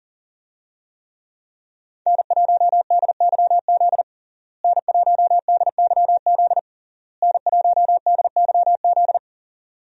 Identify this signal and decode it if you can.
Morse code, heard as a keyed tone, about 37 wpm, 700 Hz. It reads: N1DYZ N1DYZ N1DYZ
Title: USA Call Signs - 5 Characters - Repeated 3x - 40wpm